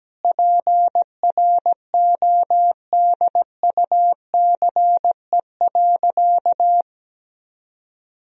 Morse code, heard as a keyed tone, about 17 wpm, 700 Hz. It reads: PRODUCE.
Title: Sentences from Top 1000 Words - Review Part 3 - 17wpm